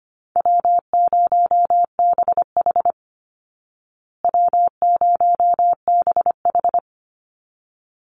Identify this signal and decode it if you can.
Morse code, heard as a keyed tone, about 25 wpm, 700 Hz. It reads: W0BH W0BH
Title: Easy Call Signs Repeated 2x - 25wpm